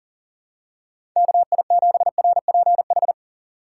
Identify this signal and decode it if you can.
Morse code, heard as a keyed tone, about 40 wpm, 700 Hz. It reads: KI7RPH
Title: USA Call Signs - 6 Characters - 4x Character Spacing - 40wpm